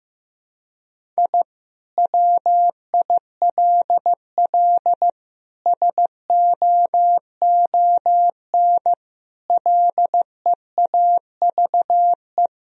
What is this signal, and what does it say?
Morse code, heard as a keyed tone, about 15 wpm, 700 Hz. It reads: I WILL SOON LEAVE
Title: Sentences from Top 1000 Words - Review Part 2 - 15wpm